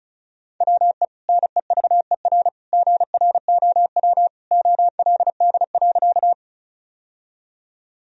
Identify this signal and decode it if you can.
Morse code, heard as a keyed tone, about 36 wpm, 700 Hz. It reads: WE NEVER GROW OLD.
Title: Sentences from Top 1000 Words - Review Part 1 - 35wpm